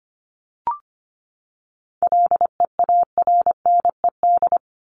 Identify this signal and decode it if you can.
Morse code, heard as a keyed tone, about 25 wpm, 700 Hz. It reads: LEARNED
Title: ED Suffix - Top 100 Words with Base Words 25wpm